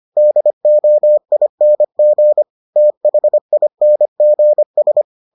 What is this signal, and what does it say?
Morse code, heard as a keyed tone, about 25 wpm, 600 Hz. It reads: DOING THINGS